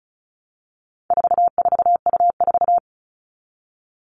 Morse code, heard as a keyed tone, about 34 wpm, 700 Hz. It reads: 44U4
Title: All Cut Numbers - 4-Digit 35wpm